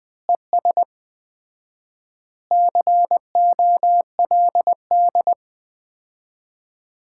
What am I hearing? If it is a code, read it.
Morse code, heard as a keyed tone, about 21 wpm, 700 Hz. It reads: ES COLD
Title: Common QSO Phrases - 4x Word-Spacing - 20wpm